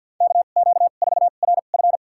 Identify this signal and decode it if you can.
Morse code, heard as a keyed tone, about 51 wpm, spaced as for 36 wpm, 700 Hz. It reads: KQ4RF
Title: USA Call Signs - 5 Characters - 2x Character Spacing - 50wpm